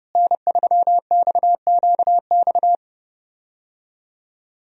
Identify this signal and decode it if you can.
Morse code, heard as a keyed tone, about 32 wpm, 700 Hz. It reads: N3XQX
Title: USA Call Signs - 5 Characters - Repeated 2x - 30wpm